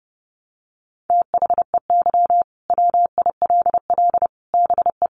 Morse code, heard as a keyed tone, about 30 wpm, 700 Hz. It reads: THEY WILL BE
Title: Sets of 5 Words Encore 30wpm